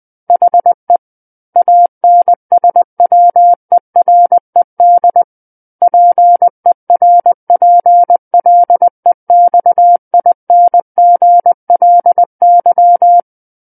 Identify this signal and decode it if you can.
Morse code, heard as a keyed tone, about 20 wpm, 700 Hz. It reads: HE ANSWERED PERPLEXINGLY
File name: 20 WPM